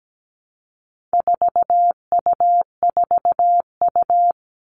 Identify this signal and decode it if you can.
Morse code, heard as a keyed tone, about 17 wpm, 700 Hz. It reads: 4U4U
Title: All Cut Numbers - 4-Digit 17wpm